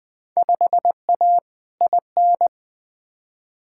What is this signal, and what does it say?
Morse code, heard as a keyed tone, about 21 wpm, 700 Hz. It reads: 5A IN
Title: Field Day Exchanges Encore - 20wpm